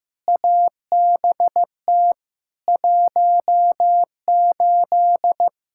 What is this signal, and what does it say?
Morse code, heard as a keyed tone, about 15 wpm, 700 Hz. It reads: ABT 18
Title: K1USN SST QSO Practice 15wpm